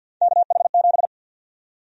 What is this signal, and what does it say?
Morse code, heard as a keyed tone, about 52 wpm, 700 Hz. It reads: KH7
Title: USA Call Sign Prefixes - 50wpm